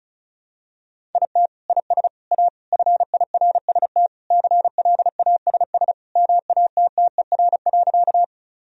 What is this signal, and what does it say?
Morse code, heard as a keyed tone, about 34 wpm, 700 Hz. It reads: IT IS A FIRST CLASS MATTER.
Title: Sentences from Top 1000 Words - Review Part 2 - 35wpm